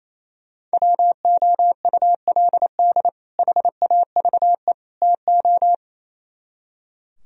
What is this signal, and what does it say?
Morse code, heard as a keyed tone, about 33 wpm, 700 Hz. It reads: WOULD HAVE TO
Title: Speed-Racing - Sets of 3 Words 28wpm